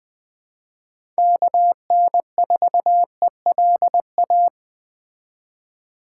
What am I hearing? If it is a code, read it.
Morse code, heard as a keyed tone, about 20 wpm, 700 Hz. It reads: KN4ELA
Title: Speed Racing - USA 2x3 Call Signs 17wpm